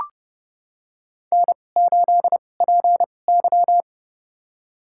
Morse code, heard as a keyed tone, about 32 wpm, spaced as for 22 wpm, 700 Hz. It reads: N8PY
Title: USA Call Signs - 4 Characters - 2x Character Spacing - 30wpm